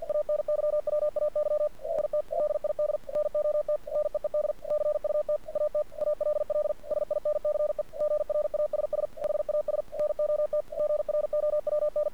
Morse code, heard as a keyed tone, about 48 wpm, 600 Hz. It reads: ANYWAY IT DID NOT NEED CUT AT ALL SINCE GRASS HAS NOT GROWN
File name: archived / CRSnet